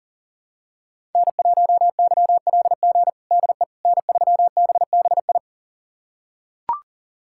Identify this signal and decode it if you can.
Morse code, heard as a keyed tone, about 40 wpm, 700 Hz. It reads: N1YLG DE N3BBI
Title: USA Callsign DE Callsign - 5 Characters - 40wpm